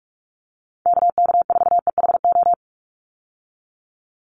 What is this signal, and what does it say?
Morse code, heard as a keyed tone, about 44 wpm, 700 Hz. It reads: KK4EHO
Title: USA Call Signs 45wpm